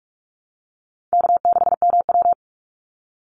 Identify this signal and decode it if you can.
Morse code, heard as a keyed tone, about 44 wpm, 690 Hz. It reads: K6MW
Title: NA SOTA Chaser Callsigns 45wpm